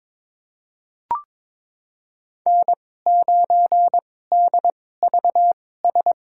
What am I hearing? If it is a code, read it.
Morse code, heard as a keyed tone, about 22 wpm, spaced as for 16 wpm, 700 Hz. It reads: N9DVS
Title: USA Call Signs - 5 Characters - 2x Character Spacing - 22wpm